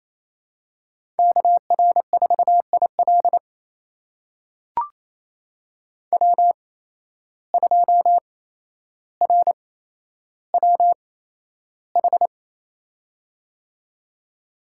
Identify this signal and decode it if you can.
Morse code, heard as a keyed tone, about 27 wpm, spaced as for 12 wpm, 700 Hz. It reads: KR4IL W2RWH
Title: USA Call Signs - 5 Characters - 8x Character Spacing - 28wpm